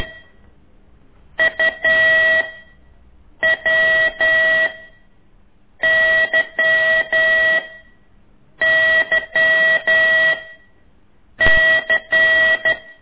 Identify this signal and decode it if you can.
Morse code, heard as a keyed tone, about 12 wpm, spaced as for 6 wpm, 1950 Hz, with noise. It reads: UWYYC
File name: ik1weg2 / CORSI CW